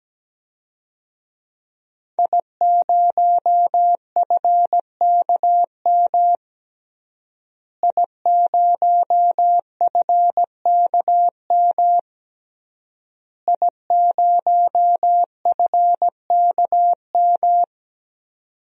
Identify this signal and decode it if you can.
Morse code, heard as a keyed tone, about 17 wpm, 700 Hz. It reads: I0FKM I0FKM I0FKM
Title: DXCC-100 Call Signs - 5 Characters - Repeated 3x - 17wpm